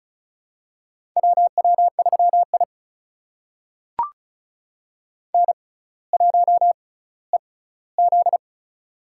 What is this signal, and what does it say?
Morse code, heard as a keyed tone, about 34 wpm, spaced as for 18 wpm, 700 Hz. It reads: WW3I N1EZ
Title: USA Call Signs - 4 Characters - 6x Character Spacing - 35wpm